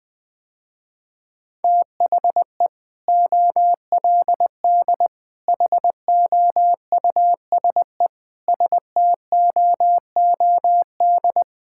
Morse code, heard as a keyed tone, about 21 wpm, 700 Hz. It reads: THE OLD HOUSE STOOD
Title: Sentences with 2-Word Phrases - Spoken Beginning - 20wpm